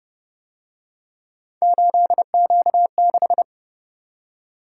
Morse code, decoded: 8Q6

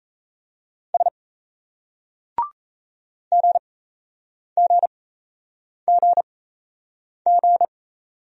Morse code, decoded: S GGGG

